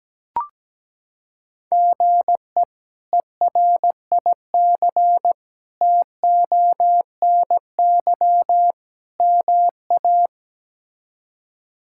Morse code, heard as a keyed tone, about 17 wpm, 700 Hz. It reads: GE ERIC TONY MA
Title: K1USN SST QSO Practice - 17wpm